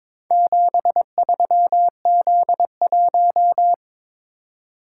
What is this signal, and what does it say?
Morse code, heard as a keyed tone, about 22 wpm, 700 Hz. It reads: 73Z1
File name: practice / 22wpm / Four Letter-Number Combinations - 22wpm